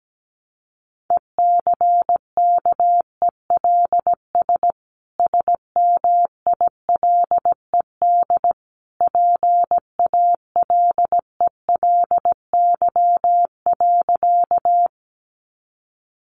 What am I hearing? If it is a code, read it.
Morse code, heard as a keyed tone, about 17 wpm, 700 Hz. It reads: ECKELS SMILED PALELY.
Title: A Sound Of Thunder 17wpm